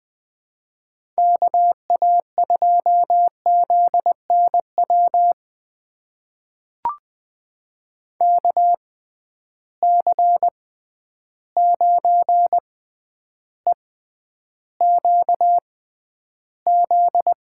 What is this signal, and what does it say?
Morse code, heard as a keyed tone, about 20 wpm, spaced as for 10 wpm, 700 Hz. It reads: KA2ZNW KC9EQZ
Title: USA Call Signs - 6 Characters - 6x Character Spacing - 20wpm